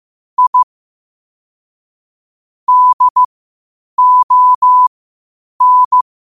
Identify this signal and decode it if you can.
Morse code, heard as a keyed tone, about 15 wpm, spaced as for 8 wpm, 1000 Hz. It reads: I DON
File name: qotdfiles / 8wpm